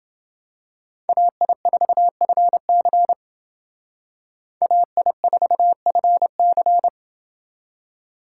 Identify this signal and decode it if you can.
Morse code, heard as a keyed tone, about 29 wpm, 700 Hz. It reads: AI4FC AI4FC